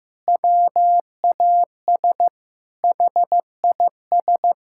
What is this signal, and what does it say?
Morse code, heard as a keyed tone, about 16 wpm, 700 Hz. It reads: WAS HIS
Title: Sentences from Top 1000 Words - Review Part 6 - 15wpm